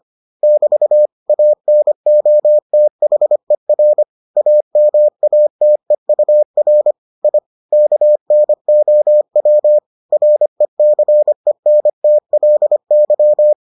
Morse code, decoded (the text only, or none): <BT> ANOTHER AMATEUR I KNOW RECENTLY